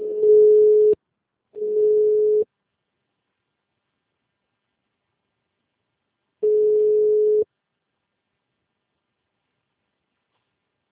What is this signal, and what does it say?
Ringback tone (Europe, 425 Hz).